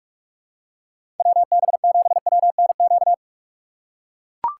Morse code, heard as a keyed tone, about 44 wpm, 700 Hz. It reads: WB7WNQ
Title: USA Call Signs - 6 Characters - 6x Character Spacing - 45wpm